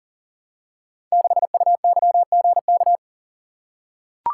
Morse code, heard as a keyed tone, about 42 wpm, 700 Hz. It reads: 6UYGK
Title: Code Groups - 5 Letter:Numbers - 40wpm